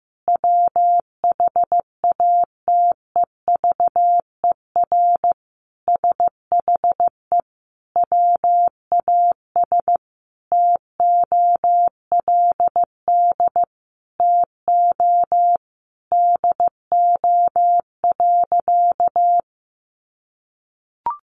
Morse code, heard as a keyed tone, about 15 wpm, 700 Hz. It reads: WHATEVER SHE WAS TOLD TO DO.
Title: Sentences from Top 1100 Words 15wpm